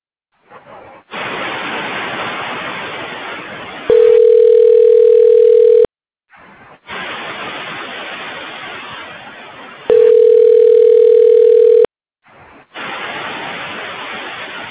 Ringback tone (North America), 440 + 480 Hz.